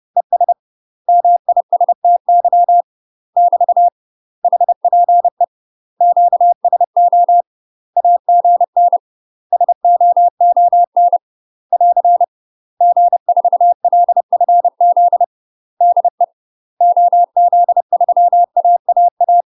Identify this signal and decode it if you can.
Morse code, heard as a keyed tone, about 30 wpm, 700 Hz. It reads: ES MISTY <BT> HPE QSO AGN SOON <AR> G4LFZ DE OZ3AAA
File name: UK MASTER QSOs TRAINING / 30WPM / UK MASTER QSOs 0046